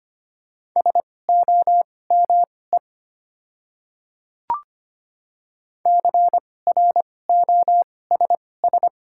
Morse code, heard as a keyed tone, about 25 wpm, spaced as for 18 wpm, 700 Hz. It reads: SOME CROSS
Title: 2x Character Spacing - Top 300 Words - 25wpm